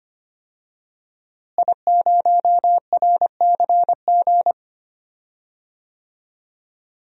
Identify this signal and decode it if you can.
Morse code, heard as a keyed tone, about 25 wpm, 700 Hz. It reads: I0RCG
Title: DXCC-100 Call Signs - 5 Characters - Repeated 2x - 25wpm